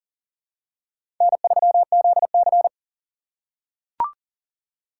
Morse code, heard as a keyed tone, about 38 wpm, 700 Hz. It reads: N3ZC
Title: USA Call Signs - 4 Characters - 3x Character Spacing - 40wpm